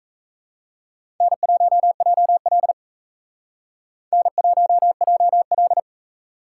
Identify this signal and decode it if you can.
Morse code, heard as a keyed tone, about 35 wpm, 700 Hz. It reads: N1JL N1JL
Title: Speed Racing - USA 1x2 Call Signs 28wpm